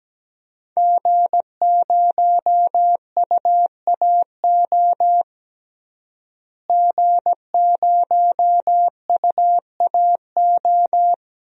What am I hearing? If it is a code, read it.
Morse code, heard as a keyed tone, about 17 wpm, 700 Hz. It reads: G0UAO G0UAO
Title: DXCC-100 Call Signs - 5 Characters - Repeated 2x - 17wpm